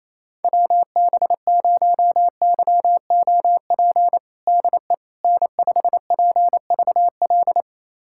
Morse code, heard as a keyed tone, about 28 wpm, 700 Hz. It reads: WB0YOP DE N5PVL
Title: USA Callsign DE Callsign 28wpm